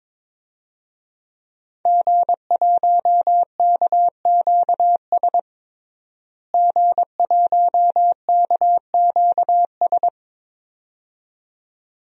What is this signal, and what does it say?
Morse code, heard as a keyed tone, about 22 wpm, 700 Hz. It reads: G1KQS G1KQS
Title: DXCC-100 Call Signs - 5 Characters - Repeated 2x - 22wpm